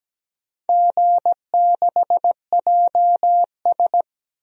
Morse code, heard as a keyed tone, about 17 wpm, 700 Hz. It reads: G6JS
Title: DXCC-100 Call Signs - 4 Characters - 17wpm